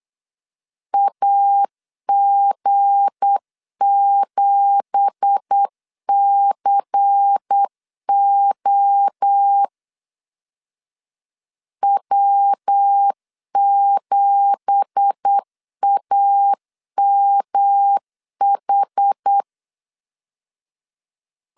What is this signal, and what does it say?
Morse code, heard as a keyed tone, about 8 wpm, 800 Hz. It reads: AG7CO W7AMH
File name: CW / DET / 1 9 1500C S C 8wpm 5653